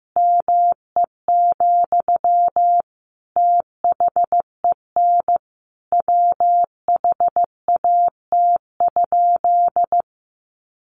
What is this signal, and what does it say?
Morse code, heard as a keyed tone, about 15 wpm, 700 Hz. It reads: ME, THEN WHAT?
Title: Sentences from Top 1300 Words 15wpm